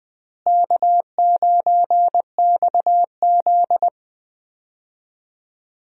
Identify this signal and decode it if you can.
Morse code, heard as a keyed tone, about 20 wpm, 700 Hz. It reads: K9XZ
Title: USA Call Signs - 4 Characters - Repeated 3x - 20wpm